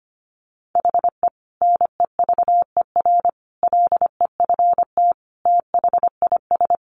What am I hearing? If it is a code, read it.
Morse code, heard as a keyed tone, about 25 wpm, 700 Hz. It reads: HE NEVER LEFT THIS
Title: Sentences from Top 1100 Words 25wpm